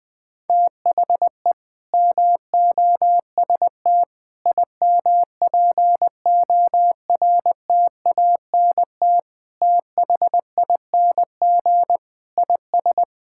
Morse code, heard as a keyed tone, about 21 wpm, 700 Hz. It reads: THE MOST IMPORTANT THING IS
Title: Sentences with 3-Word Phrases - 20wpm